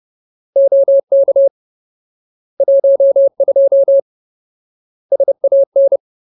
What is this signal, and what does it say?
Morse code, heard as a keyed tone, about 30 wpm, 550 Hz. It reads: OK 12 SAN
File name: Ultimate Word List / 30WPM Ultimate